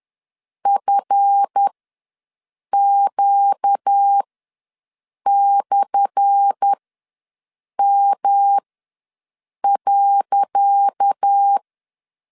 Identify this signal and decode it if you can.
Morse code, heard as a keyed tone, about 10 wpm, spaced as for 6 wpm, 800 Hz. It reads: FQ/M.
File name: LEZIONI / 10 SIGNS / 10C DOT ws 10wpm 4213